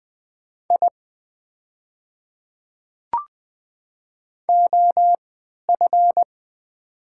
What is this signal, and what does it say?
Morse code, heard as a keyed tone, about 20 wpm, spaced as for 11 wpm, 700 Hz. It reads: I OF
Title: 3x Character Spacing - Top 200 Words - 20wpm